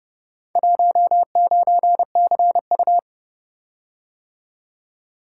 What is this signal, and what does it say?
Morse code, heard as a keyed tone, about 30 wpm, 700 Hz. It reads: 19CU